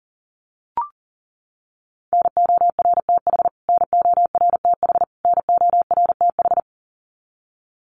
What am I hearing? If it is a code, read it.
Morse code, heard as a keyed tone, about 40 wpm, 700 Hz. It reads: NORTH NORTH NORTH
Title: Top 500 Words Repeated 40wpm